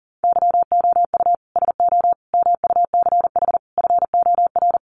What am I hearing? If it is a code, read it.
Morse code, heard as a keyed tone, about 40 wpm, 700 Hz. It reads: YOU SO MUCH FOR
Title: Sets of 5 Words Encore2 40wpm